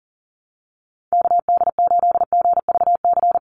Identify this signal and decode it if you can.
Morse code, heard as a keyed tone, about 40 wpm, 700 Hz. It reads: KD8GVC
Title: USA Call Signs 40wpm